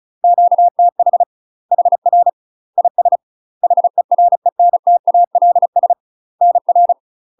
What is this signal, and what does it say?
Morse code, heard as a keyed tone, about 35 wpm, 700 Hz. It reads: QTH HR IS HERENTALS NR
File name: UK MASTER QSOs TRAINING / 35WPM / UK MASTER QSOs 0035